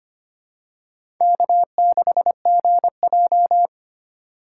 Morse code, heard as a keyed tone, about 25 wpm, 690 Hz. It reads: K6GJ
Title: USA Call Signs - 4x Character Spacing - 25wpm